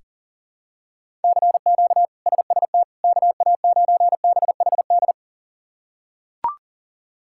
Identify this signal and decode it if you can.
Morse code, heard as a keyed tone, about 40 wpm, 700 Hz. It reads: CQ SST KA9BHD
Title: K1USN SST QSO Practice - 40wpm